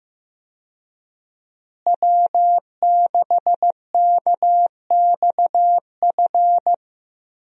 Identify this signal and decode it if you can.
Morse code, heard as a keyed tone, about 15 wpm, 700 Hz. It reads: W6KXF
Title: USA Call Signs - 5 Characters - Repeated 2x - 15wpm